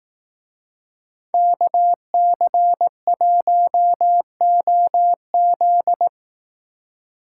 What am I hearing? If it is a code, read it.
Morse code, heard as a keyed tone, about 18 wpm, 700 Hz. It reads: KC1OZ
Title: Speed Racing - USA 2x2 Call Signs 15wpm